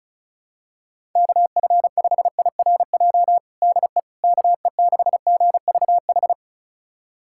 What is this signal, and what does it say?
Morse code, heard as a keyed tone, about 36 wpm, 700 Hz. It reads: KF5IRJ DE KE6GVH
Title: USA Callsign DE Callsign - 6 Characters - 35wpm